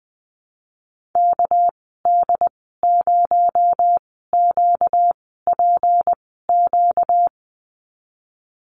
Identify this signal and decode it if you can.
Morse code, heard as a keyed tone, about 19 wpm, spaced as for 14 wpm, 700 Hz. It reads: KD0QPQ
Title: 245 Lesson 39g - Letter z - Call Signs 20wpm